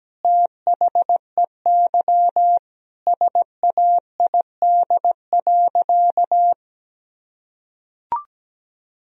Morse code, heard as a keyed tone, about 17 wpm, 700 Hz. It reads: THEY SAID.
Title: Sentences from Top 1000 Words - Review Part 2 - 17wpm